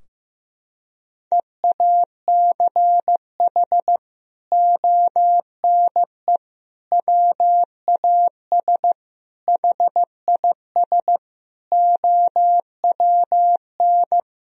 Morse code, heard as a keyed tone, about 15 wpm, 700 Hz. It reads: EACH ONE WAS HIS OWN
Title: Sentences from Top 1000 Words - Review Part 4 - 15wpm